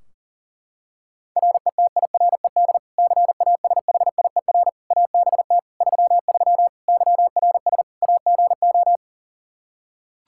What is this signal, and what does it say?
Morse code, heard as a keyed tone, about 40 wpm, 700 Hz. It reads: RETIRED CASHIER ABT 33 YRS AGO
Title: Practice QSO - Exchange 3 - More Info - 40wpm